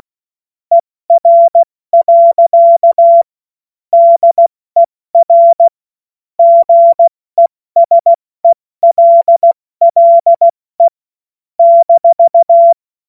Morse code, heard as a keyed tone, about 16 wpm, 680 Hz. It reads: ER. DER GESELLE -
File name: Morsekurs / Witze / Witz 20. Januar 2026 70 F80